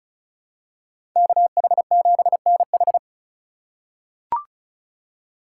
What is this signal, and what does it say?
Morse code, heard as a keyed tone, about 34 wpm, 700 Hz. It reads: KH7NH
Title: USA Call Signs - 2x Character Spacing - 35wpm